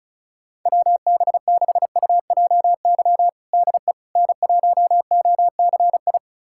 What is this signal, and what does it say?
Morse code, heard as a keyed tone, about 34 wpm, 700 Hz. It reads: WB6UJY DE N1OCI
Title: USA Callsign DE Callsign 35wpm